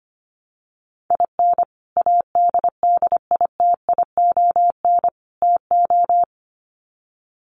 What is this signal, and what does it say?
Morse code, heard as a keyed tone, about 25 wpm, 700 Hz. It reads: IN ADDITION TO
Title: Sets of 3 Words 25wpm